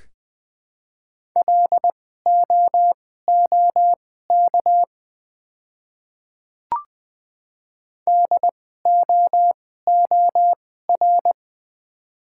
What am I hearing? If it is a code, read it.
Morse code, heard as a keyed tone, about 20 wpm, spaced as for 14 wpm, 700 Hz. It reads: LOOK DOOR
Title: 2x Character Spacing - Top 500 Words - 20wpm